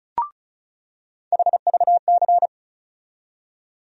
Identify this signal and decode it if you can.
Morse code, heard as a keyed tone, about 36 wpm, 700 Hz. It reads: HVC
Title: USA Call Sign Postfixes - 3 Letters 35wpm